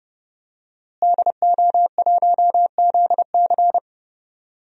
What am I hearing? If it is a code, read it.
Morse code, heard as a keyed tone, about 30 wpm, 700 Hz. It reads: DO1ZC
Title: DXCC-100 Call Signs - 5 Characters - Repeated 3x - 30wpm